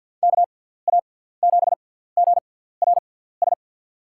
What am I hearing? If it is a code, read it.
Morse code, heard as a keyed tone, about 47 wpm, spaced as for 17 wpm, 700 Hz. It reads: KA7GRS
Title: USA Call Signs - 6x Character Spacing - 50wpm